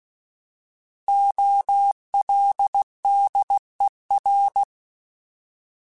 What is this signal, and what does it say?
Morse code, heard as a keyed tone, about 16 wpm, 800 Hz. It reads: OLDER
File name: Alphabetical / Alphabetical mp3s / Os @15